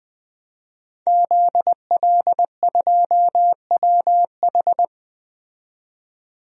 Morse code, heard as a keyed tone, about 20 wpm, 700 Hz. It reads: ZL2WH
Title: DXCC-100 Call Signs - 5 Characters - 20wpm